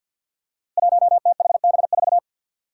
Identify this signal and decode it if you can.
Morse code, heard as a keyed tone, about 53 wpm, 700 Hz. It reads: 1THB4
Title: Five Challenging Letter-Number Combinations 50wpm